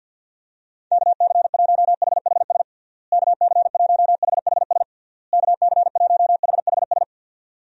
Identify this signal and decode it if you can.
Morse code, heard as a keyed tone, about 51 wpm, 700 Hz. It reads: KC1HHS KC1HHS KC1HHS
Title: USA Call Signs - 6 Characters - Repeated 3x - 50wpm